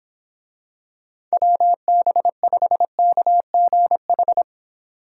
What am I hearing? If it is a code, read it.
Morse code, heard as a keyed tone, about 24 wpm, 700 Hz. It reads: WB5KGH